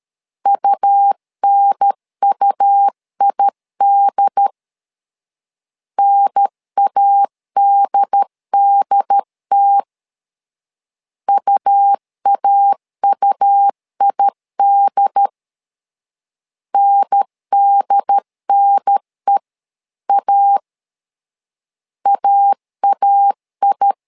Morse code, heard as a keyed tone, about 13 wpm, 800 Hz. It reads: UNUID NADDT UAUID NDNE A AAI
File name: LEZIONI / 2 UHDN / 2 UHDN 12wpm 1456